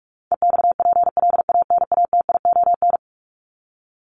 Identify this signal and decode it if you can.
Morse code, heard as a keyed tone, about 44 wpm, 700 Hz. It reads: EXPLANATION
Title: Top 1751-2000 Words 45wpm